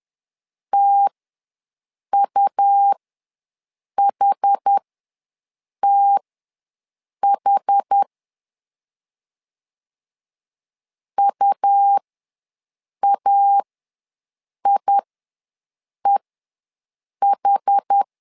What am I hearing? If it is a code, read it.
Morse code, heard as a keyed tone, about 10 wpm, spaced as for 6 wpm, 800 Hz. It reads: TUHTH UAIEH
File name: LEZIONI / 2 UHDN / 2 UH ws 10wpm 2029